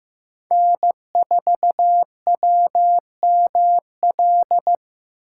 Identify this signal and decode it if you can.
Morse code, heard as a keyed tone, about 18 wpm, 700 Hz. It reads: N4WML
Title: Speed Racing - USA 1x3 Call Signs 15wpm